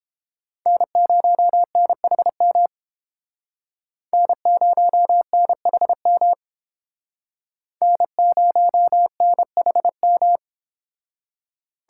Morse code, heard as a keyed tone, about 30 wpm, 700 Hz. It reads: N0NHM N0NHM N0NHM